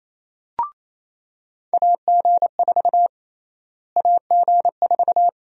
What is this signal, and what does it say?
Morse code, heard as a keyed tone, about 28 wpm, 700 Hz. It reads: AG4 AG4
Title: USA Call Sign Prefixes - Repeated 2x - 28wpm